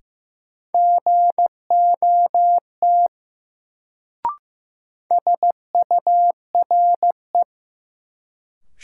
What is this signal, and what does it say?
Morse code, heard as a keyed tone, about 15 wpm, 700 Hz. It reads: GOT SURE
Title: Sentences from Top 300 Words - New Word Drill - 15wpm